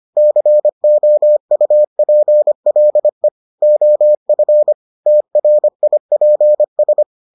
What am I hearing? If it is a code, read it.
Morse code, heard as a keyed tone, about 25 wpm, 600 Hz. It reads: COUPLE OF TRIPS